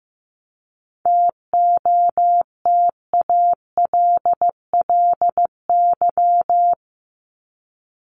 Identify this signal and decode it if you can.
Morse code, heard as a keyed tone, about 15 wpm, 700 Hz. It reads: TOTALLY
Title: LY Suffix - Top 100 Words with Base Words 15wpm